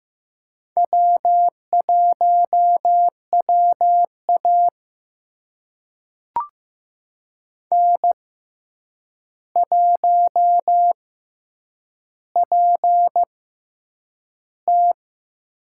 Morse code, heard as a keyed tone, about 15 wpm, spaced as for 8 wpm, 700 Hz. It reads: W1WA N1PT
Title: USA Call Signs - 4 Characters - 6x Character Spacing - 15wpm